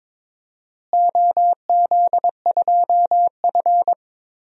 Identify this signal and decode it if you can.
Morse code, heard as a keyed tone, about 22 wpm, 700 Hz. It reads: OZ2F
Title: DXCC-100 Call Signs - 4 Characters - Repeated 3x - 22wpm